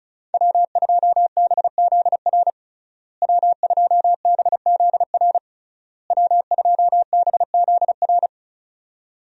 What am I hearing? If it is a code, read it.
Morse code, heard as a keyed tone, about 36 wpm, 700 Hz. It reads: W2BZR W2BZR W2BZR
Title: USA Call Signs - 5 Characters - Repeated 3x - 35wpm